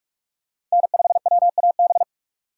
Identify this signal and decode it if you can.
Morse code, heard as a keyed tone, about 46 wpm, 700 Hz. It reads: N5WAB
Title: USA Call Signs - 5 Characters - 2x Character Spacing - 45wpm